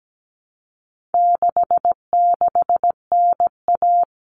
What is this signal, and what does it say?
Morse code, heard as a keyed tone, about 17 wpm, 700 Hz. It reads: 66NA
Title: All Cut Numbers - 4-Digit 17wpm